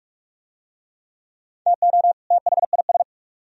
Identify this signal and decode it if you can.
Morse code, heard as a keyed tone, about 44 wpm, 700 Hz. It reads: TO THIS